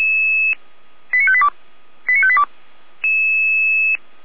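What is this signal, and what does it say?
Five-tone selective call (ZVEI), heard as quick runs of five tones, tones of about 80 ms. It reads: 89582 89582
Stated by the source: ZVEI five-tone selective call 89582 received by SDR, logged by feuerwehr-orlamuende fire brigade server.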